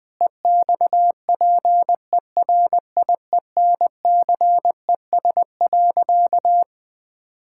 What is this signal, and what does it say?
Morse code, heard as a keyed tone, about 20 wpm, 700 Hz. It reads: EXPERIENCES.
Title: Difficult Sentences with Hints - 20wpm